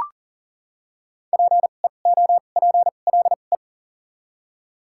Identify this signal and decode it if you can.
Morse code, heard as a keyed tone, about 39 wpm, spaced as for 29 wpm, 700 Hz. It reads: PEOPLE
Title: 2x Character Spacing - Top 100 Words - 40wpm